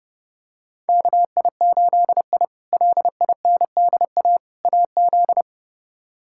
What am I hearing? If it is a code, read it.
Morse code, heard as a keyed tone, about 32 wpm, 700 Hz. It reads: KI8I LINDA AZ
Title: K1USN SST QSO Practice - 30wpm